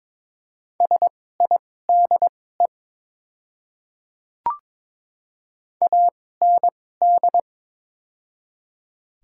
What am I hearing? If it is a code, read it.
Morse code, heard as a keyed tone, about 22 wpm, spaced as for 16 wpm, 700 Hz. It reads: SIDE AND
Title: 2x Character Spacing - Top 200 Words - 22wpm